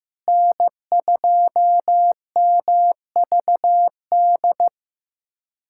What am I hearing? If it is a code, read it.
Morse code, heard as a keyed tone, about 15 wpm, 700 Hz. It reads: N2MVD
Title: USA Callsign DE Callsign - 5 Characters - 15wpm